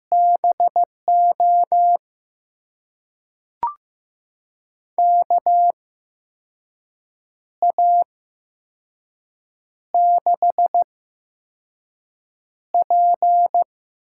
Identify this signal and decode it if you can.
Morse code, heard as a keyed tone, about 15 wpm, spaced as for 5 wpm, 700 Hz. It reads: BO KA6P
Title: USA Call Signs - 6 Characters - 8x Character Spacing - 15wpm